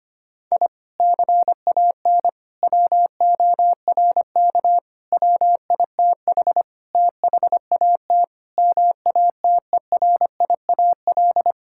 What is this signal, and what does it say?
Morse code, heard as a keyed tone, about 25 wpm, 700 Hz. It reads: I CAN WORK WITH THAT MATERIAL
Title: Sentences from Top 1000 Words - Review Part 5 - 25wpm